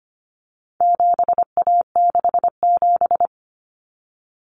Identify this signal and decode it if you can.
Morse code, heard as a keyed tone, about 25 wpm, 700 Hz. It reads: 7A67
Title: Common Cut Numbers - 4-Digit 25wpm